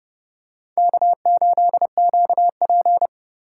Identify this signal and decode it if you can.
Morse code, heard as a keyed tone, about 28 wpm, 700 Hz. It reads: K8QP